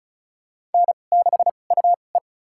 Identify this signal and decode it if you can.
Morse code, heard as a keyed tone, about 35 wpm, spaced as for 25 wpm, 700 Hz. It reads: N6UE
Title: USA Call Signs - 4 Characters - 2x Character Spacing - 35wpm